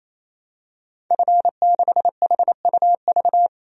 Morse code, heard as a keyed tone, about 27 wpm, 700 Hz. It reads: F6HUV
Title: DXCC-100 Call Signs - 5 Characters - 28wpm